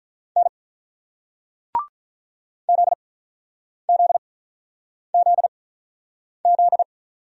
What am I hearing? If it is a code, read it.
Morse code, heard as a keyed tone, about 50 wpm, spaced as for 9 wpm, 700 Hz. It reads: N ZZZZ